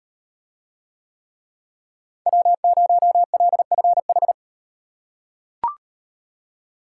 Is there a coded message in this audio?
Yes: Morse code from 2.3 to 4.3 s, heard as a keyed tone, about 38 wpm, 700 Hz. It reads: W0LFH